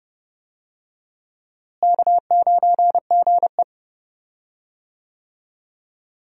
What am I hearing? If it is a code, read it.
Morse code, heard as a keyed tone, about 30 wpm, 700 Hz. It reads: K9GE
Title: USA Call Signs - 4 Characters - Repeated 3x - 30wpm